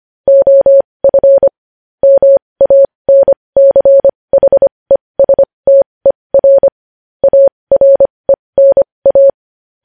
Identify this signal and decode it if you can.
Morse code, heard as a keyed tone, about 25 wpm, spaced as for 20 wpm, 550 Hz. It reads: OF MANCHESTER ARENA
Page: News Headlines in Morse Code at 20 WPM